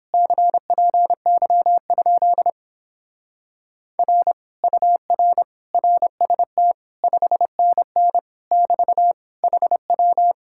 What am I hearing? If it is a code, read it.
Morse code, heard as a keyed tone, about 29 wpm, 700 Hz. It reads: CPY? R UR RST 5NN <BT> HW